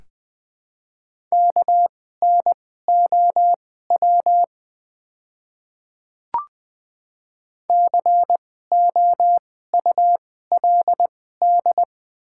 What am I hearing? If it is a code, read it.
Morse code, heard as a keyed tone, about 20 wpm, spaced as for 14 wpm, 700 Hz. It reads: KNOW COULD
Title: 2x Character Spacing - Top 100 Words - 20wpm